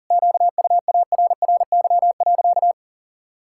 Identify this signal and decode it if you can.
Morse code, heard as a keyed tone, about 40 wpm, 700 Hz. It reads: QUARRY.
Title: Pangram Challenge v2 - 40wpm